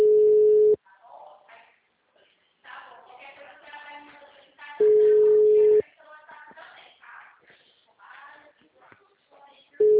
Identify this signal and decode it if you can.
Ringback tone (Europe, 425 Hz), with speech.
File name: out- -2008- - - .35827